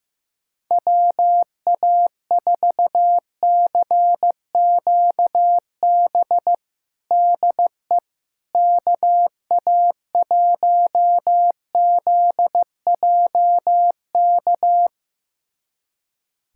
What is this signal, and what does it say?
Morse code, heard as a keyed tone, about 15 wpm, 700 Hz. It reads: WA4CQB DE KA1ZJK
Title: USA Callsign DE Callsign - 6 Characters - 15wpm